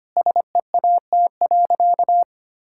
Morse code, heard as a keyed tone, about 25 wpm, 700 Hz. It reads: SEAT.
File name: practice / 25wpm / All-Age Appropriate Jokes - 25wpm